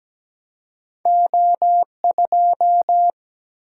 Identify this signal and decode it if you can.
Morse code, heard as a keyed tone, about 17 wpm, 700 Hz. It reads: O2
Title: Single-Letter Single-Number 17wpm